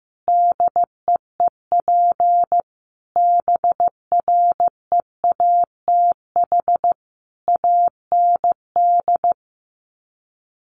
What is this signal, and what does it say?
Morse code, heard as a keyed tone, about 14 wpm, 700 Hz. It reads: DEEP BREATH AND
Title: Sets of 5 Words Encore 15wpm